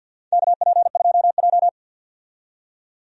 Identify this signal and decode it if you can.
Morse code, heard as a keyed tone, about 47 wpm, 700 Hz. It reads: KP2J
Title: DXCC-100 Call Signs - 4 Characters - 50wpm